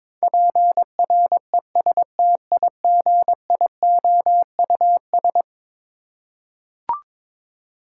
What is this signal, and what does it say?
Morse code, heard as a keyed tone, about 22 wpm, 700 Hz. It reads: PRESTIGIOUS
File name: practice / 22wpm / Eleven Letter Words - 22wpm